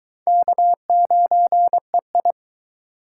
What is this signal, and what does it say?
Morse code, heard as a keyed tone, about 23 wpm, 700 Hz. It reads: K9EI